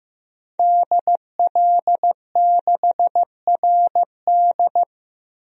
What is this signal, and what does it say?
Morse code, heard as a keyed tone, about 15 wpm, 700 Hz. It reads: DL6RD
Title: DXCC-100 Call Signs - 5 Characters - Repeated 2x - 15wpm